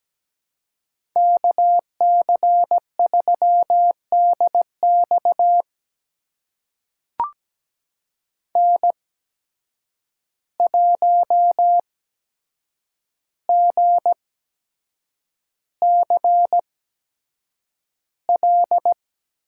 Morse code, heard as a keyed tone, about 17 wpm, spaced as for 7 wpm, 700 Hz. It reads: KC3DX N1GCL
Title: USA Call Signs - 5 Characters - 8x Character Spacing - 17wpm